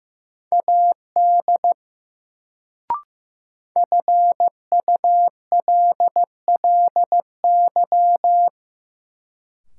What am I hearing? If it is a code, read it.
Morse code, heard as a keyed tone, about 15 wpm, 700 Hz. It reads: AD FULLY
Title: Sentences from Top 1300 Words - New Word Drill Alt - 15wpm